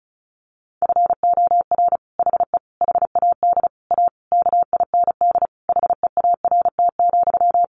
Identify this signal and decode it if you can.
Morse code, heard as a keyed tone, about 34 wpm, 700 Hz. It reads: FOR HE HAD A KIND HEART,
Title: Sentences from Top 1400 Words 35wpm